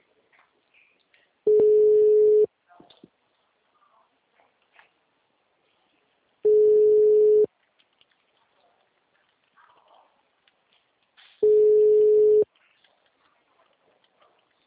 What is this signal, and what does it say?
Ringback tone (Europe, 425 Hz), with speech.
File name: out- -2017